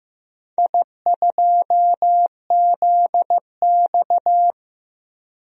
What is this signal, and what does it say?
Morse code, heard as a keyed tone, about 15 wpm, 700 Hz. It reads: I2ZX
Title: DXCC-100 Call Signs - 4 Characters - Repeated 2x - 15wpm